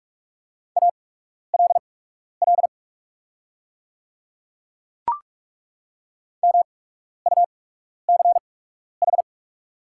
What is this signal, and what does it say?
Morse code, heard as a keyed tone, about 43 wpm, spaced as for 12 wpm, 690 Hz. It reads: ALL MUCH
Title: 8x Character Spacing - Top 200 Words - 45wpm